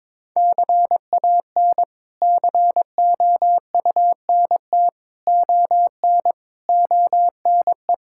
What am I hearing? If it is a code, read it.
Morse code, heard as a keyed tone, about 22 wpm, 700 Hz. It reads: CAN COUNT ON ONE
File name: practice / 22wpm / Sets of 8 Words - 22wpm